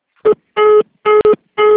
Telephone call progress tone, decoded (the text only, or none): congestion tone